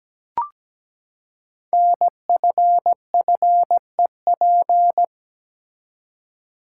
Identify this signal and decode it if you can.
Morse code, heard as a keyed tone, about 17 wpm, 700 Hz. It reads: NFFEP
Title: Code Groups - 5 Letters - 17wpm